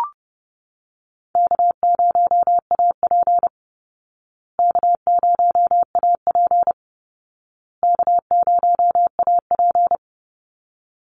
Morse code, heard as a keyed tone, about 30 wpm, 700 Hz. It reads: K0AP K0AP K0AP
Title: Easy Call Signs Repeated 3x - 30wpm